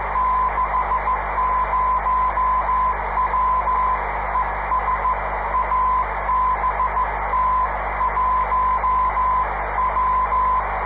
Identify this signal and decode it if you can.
Morse code, heard as a keyed tone, about 15 wpm, 1020 Hz, with radio noise. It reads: 60W ES ABT 1W
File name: 500khz / 30 december 2011 / iq2mi qso-1